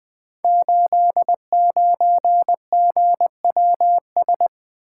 Morse code, heard as a keyed tone, about 20 wpm, 700 Hz. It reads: 89GWS
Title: Code Groups - 5 Letter:Numbers - 20wpm